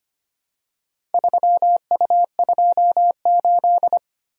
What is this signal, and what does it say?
Morse code, heard as a keyed tone, about 25 wpm, 700 Hz. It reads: 3U28
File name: practice / 25wpm / Four Letter-Number Combinations - 25wpm